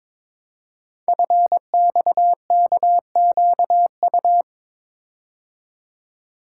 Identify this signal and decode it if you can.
Morse code, heard as a keyed tone, about 22 wpm, 700 Hz. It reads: FXKQU
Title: Code Groups - 5 Letters - 22wpm